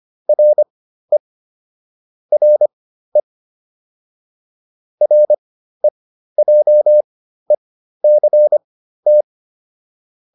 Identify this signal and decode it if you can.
Morse code, heard as a keyed tone, about 25 wpm, spaced as for 13 wpm, 600 Hz. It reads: RE RE REJECT